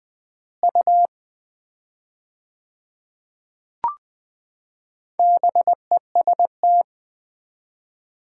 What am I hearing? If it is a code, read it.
Morse code, heard as a keyed tone, about 20 wpm, 700 Hz. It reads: U BEST
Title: Common QSO Phrases - 4x Word-Spacing - 20wpm